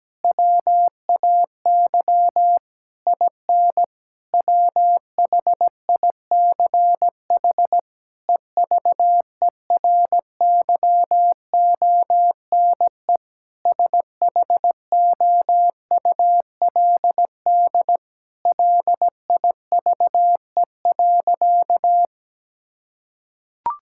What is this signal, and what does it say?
Morse code, heard as a keyed tone, about 17 wpm, 700 Hz. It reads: WAY IN WHICH EVERYONE SHOULD LIVE.
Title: Sentences from Top 1000 Words - Review Part 5 - 17wpm